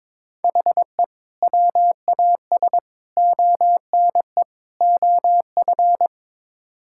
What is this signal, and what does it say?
Morse code, heard as a keyed tone, about 21 wpm, 700 Hz. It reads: HE WAS ONE OF
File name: practice / 22wpm / Build A Phrase - 5-Word Phrases - 22wpm